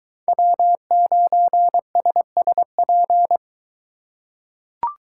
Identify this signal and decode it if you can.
Morse code, heard as a keyed tone, about 22 wpm, 700 Hz. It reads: W9SSP